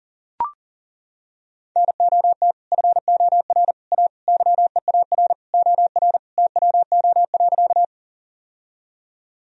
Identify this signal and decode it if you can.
Morse code, heard as a keyed tone, about 40 wpm, 700 Hz. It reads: NOT FOR A YEAR OR TWO.
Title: Sentences from Top 1000 Words - Review Part 1 - 40wpm